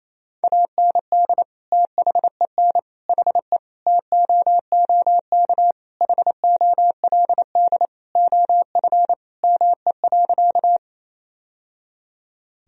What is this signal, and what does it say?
Morse code, heard as a keyed tone, about 28 wpm, 700 Hz. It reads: AND THEN HE TOOK HOLD OF ME.
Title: Sentences from Top 1000 Words - Review Part 2 - 28wpm